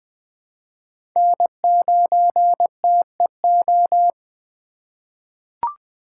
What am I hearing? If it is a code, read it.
Morse code, heard as a keyed tone, about 20 wpm, 700 Hz. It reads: N9TEO
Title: USA Call Signs - 5 Characters - 4x Character Spacing - 20wpm